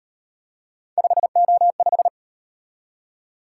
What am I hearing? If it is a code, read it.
Morse code, heard as a keyed tone, about 37 wpm, 700 Hz. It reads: 5O5